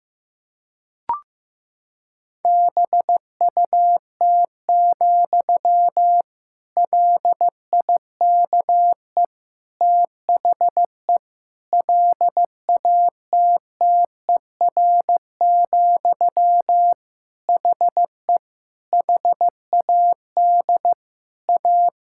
Morse code, heard as a keyed tone, about 15 wpm, 700 Hz. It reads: BUT, LIKE THE LATTER, HE HAD A
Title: Sentences from Top 1600 Words 15wpm